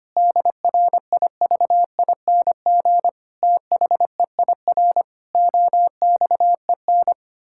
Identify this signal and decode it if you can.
Morse code, heard as a keyed tone, about 25 wpm, 700 Hz. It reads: DRIVING THEIR OXEN
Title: Pangram Challenge v2 - 25wpm